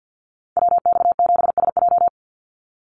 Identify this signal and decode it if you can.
Morse code, heard as a keyed tone, about 50 wpm, 700 Hz. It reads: WX7SJ
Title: Call Signs Repeated 2x - 50wpm